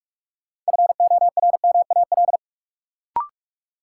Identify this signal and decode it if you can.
Morse code, heard as a keyed tone, about 44 wpm, 700 Hz. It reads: FORMAL
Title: Sentences from Top 1300 Words - New Word Drill Alt - 45wpm